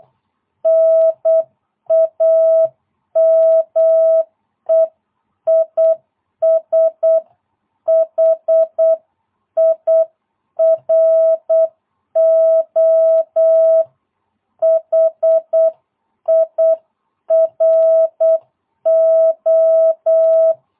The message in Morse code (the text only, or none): NAME IS HIRO HIRO